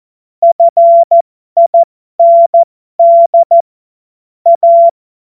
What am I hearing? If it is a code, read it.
Morse code, heard as a keyed tone, about 14 wpm, spaced as for 12 wpm, 680 Hz. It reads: FIND A
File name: Morsekurs / Witze